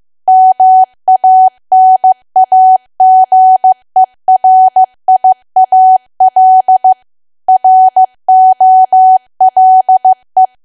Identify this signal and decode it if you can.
Morse code, heard as a keyed tone, about 15 wpm, 750 Hz. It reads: MANAGERIAL ROLE
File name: cwa / W1AW-Level2 / 15WPM